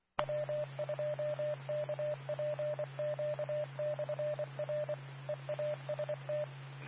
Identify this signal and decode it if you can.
Morse code, heard as a keyed tone, about 24 wpm, 600 Hz, with radio noise. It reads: W2KPQ/R EAST
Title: Nassau and Suffolk County Amateur Repeaters